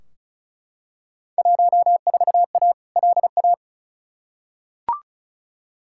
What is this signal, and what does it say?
Morse code, heard as a keyed tone, about 36 wpm, 690 Hz. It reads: 14A LA
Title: Field Day Exchanges Encore - 35wpm